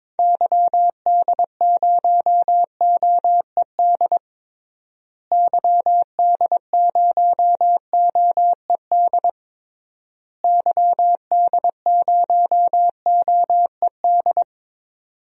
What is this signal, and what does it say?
Morse code, heard as a keyed tone, about 22 wpm, 700 Hz. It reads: YD0OED YD0OED YD0OED
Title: DXCC-100 Call Signs - 6 Characters - Repeated 3x - 22wpm